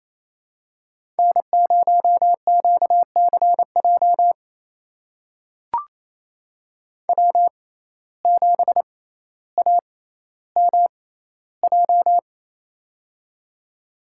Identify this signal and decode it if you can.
Morse code, heard as a keyed tone, about 28 wpm, spaced as for 14 wpm, 700 Hz. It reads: N0QCJ W7AMJ
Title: USA Call Signs - 5 Characters - 6x Character Spacing - 28wpm